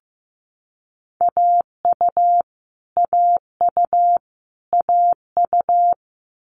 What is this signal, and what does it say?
Morse code, heard as a keyed tone, about 15 wpm, 700 Hz. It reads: AU AU AU
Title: Multi-letter Phonemes Repeated 3x 15wpm